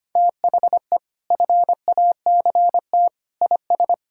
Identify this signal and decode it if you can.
Morse code, heard as a keyed tone, about 25 wpm, 700 Hz. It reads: THE FACT IS
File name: practice / 25wpm / Sentences with 2-Word Phrases - 25wpm